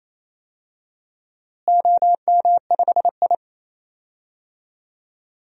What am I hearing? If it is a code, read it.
Morse code, heard as a keyed tone, about 28 wpm, 700 Hz. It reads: OM5I
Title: DXCC-100 Call Signs - 4 Characters - Repeated 3x - 28wpm